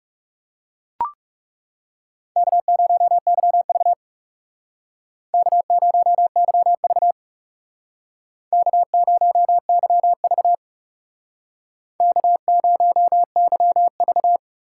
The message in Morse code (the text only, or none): K0YV K0YV K0YV K0YV